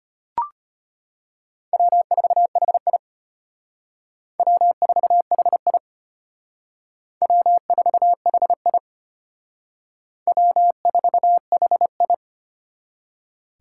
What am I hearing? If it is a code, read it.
Morse code, heard as a keyed tone, about 32 wpm, 700 Hz. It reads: W4HI W4HI W4HI W4HI